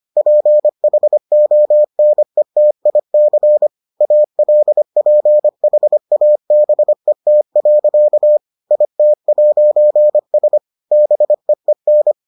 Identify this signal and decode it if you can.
Morse code, heard as a keyed tone, about 25 wpm, 600 Hz. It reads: PHONETIC ALPHABET. IT'S BEEN